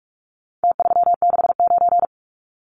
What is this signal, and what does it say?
Morse code, heard as a keyed tone, about 44 wpm, 700 Hz. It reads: T369
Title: Numbers - Contest Style - 4-Digit 45wpm